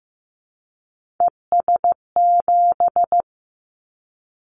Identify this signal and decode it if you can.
Morse code, heard as a keyed tone, about 15 wpm, 700 Hz. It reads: ES7
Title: Callsign Prefixes for Top 50 DXCC - 15wpm